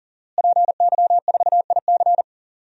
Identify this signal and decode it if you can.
Morse code, heard as a keyed tone, about 37 wpm, 700 Hz. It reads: PY4IC
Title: DXCC-100 Call Signs - 5 Characters - Repeated 2x - 40wpm